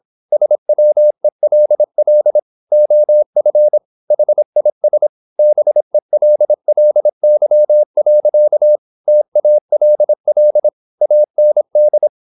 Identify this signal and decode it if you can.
Morse code, heard as a keyed tone, about 26 wpm, 600 Hz. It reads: SWELL OF HIS BELLY. TALL AND
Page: Never 26wpm | CW med Gnister